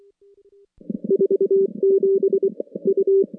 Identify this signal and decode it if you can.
Morse code, heard as a keyed tone, about 24 wpm, 410 Hz. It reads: 47 U
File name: contests / CW-SS-k5tr- -2200